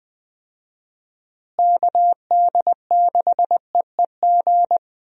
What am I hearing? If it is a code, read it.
Morse code, heard as a keyed tone, about 20 wpm, 700 Hz. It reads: KD6EEG
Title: USA Call Signs - 6 Characters - Repeated 2x - 20wpm